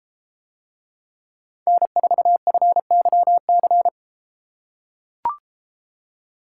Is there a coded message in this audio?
Yes: Morse code from 1.7 to 3.9 s, heard as a keyed tone, about 34 wpm, 700 Hz. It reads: N4FYC